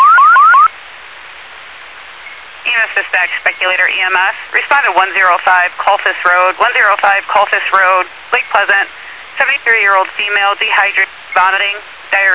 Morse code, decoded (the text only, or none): none